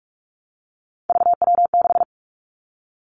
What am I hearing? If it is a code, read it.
Morse code, heard as a keyed tone, about 46 wpm, 700 Hz. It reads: VW6
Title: Three Challenging Letter-Number Combinations 45wpm